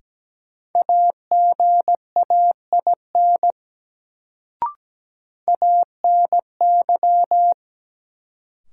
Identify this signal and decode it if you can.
Morse code, heard as a keyed tone, about 17 wpm, 700 Hz. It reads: AGAIN ANY
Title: Sentences from Top 200 Words - New Word Drill - 17wpm